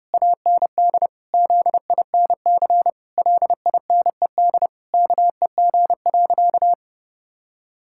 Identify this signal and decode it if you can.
Morse code, heard as a keyed tone, about 30 wpm, 700 Hz. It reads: AND ZINC LINED KEG.
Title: Pangram Challenge v2 - 30wpm